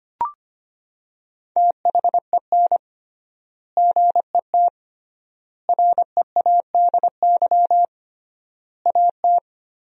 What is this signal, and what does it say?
Morse code, heard as a keyed tone, about 25 wpm, 700 Hz. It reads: THEN GET READY AT
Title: 3x Word-Spacing - Sentences from Top 400 Words - 25wpm